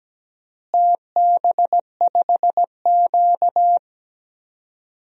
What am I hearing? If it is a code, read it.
Morse code, heard as a keyed tone, about 17 wpm, 700 Hz. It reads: TB5Q
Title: DXCC-100 Call Signs - 4 Characters - Repeated 3x - 17wpm